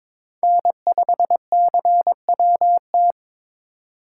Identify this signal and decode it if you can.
Morse code, heard as a keyed tone, about 26 wpm, 700 Hz. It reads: N5CWT
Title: Speed Racing - USA 1x3 Call Signs 22wpm